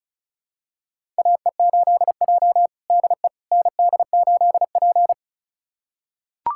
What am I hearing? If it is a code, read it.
Morse code, heard as a keyed tone, about 36 wpm, 700 Hz. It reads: AE8J DE ND8P
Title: USA Callsign DE Callsign - 4 Characters - 35wpm